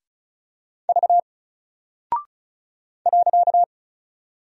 Morse code, heard as a keyed tone, about 36 wpm, 700 Hz. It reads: V .
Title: Single Character (Letter:Number:Punctuation:Prosign) Alt 35wpm